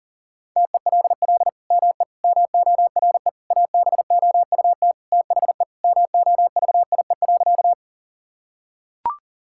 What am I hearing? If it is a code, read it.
Morse code, heard as a keyed tone, about 40 wpm, 700 Hz. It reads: TELL ME MORE ABOUT THE MOVIE.
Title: Sentences from Top 1000 Words - Review Part 5 - 40wpm